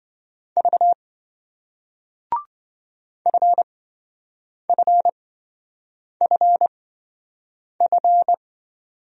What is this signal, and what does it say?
Morse code, heard as a keyed tone, about 27 wpm, spaced as for 7 wpm, 700 Hz. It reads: V FFFF